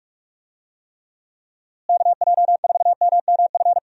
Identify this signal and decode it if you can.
Morse code, heard as a keyed tone, about 46 wpm, 700 Hz. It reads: KJ4MMF
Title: USA Call Signs - 6 Characters - Repeated 3x - 45wpm